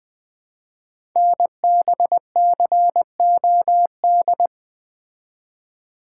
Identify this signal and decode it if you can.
Morse code, heard as a keyed tone, about 20 wpm, 700 Hz. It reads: NBCOD
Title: Code Groups - 5 Letters - 20wpm